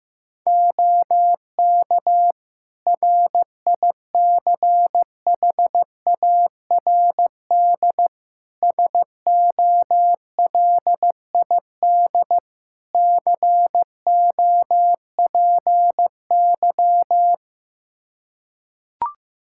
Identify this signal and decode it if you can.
Morse code, heard as a keyed tone, about 15 wpm, 700 Hz. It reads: OK RICHARD SOLID COPY
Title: Practice QSO - Exchange 3 - More Info - 15wpm